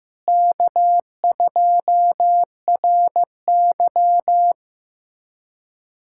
Morse code, heard as a keyed tone, about 16 wpm, 700 Hz. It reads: K2RY
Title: USA Callsign DE Callsign 15wpm